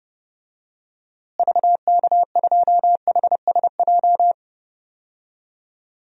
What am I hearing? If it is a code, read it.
Morse code, heard as a keyed tone, about 30 wpm, 700 Hz. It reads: VK2HSJ
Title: DXCC-100 Call Signs - 6 Characters - Repeated 3x - 30wpm